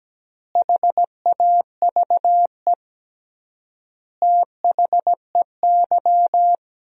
Morse code, heard as a keyed tone, about 17 wpm, 700 Hz. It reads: HAVE THEY
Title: 3x Word-Spacing - Sentences from Top 100 Words - 17wpm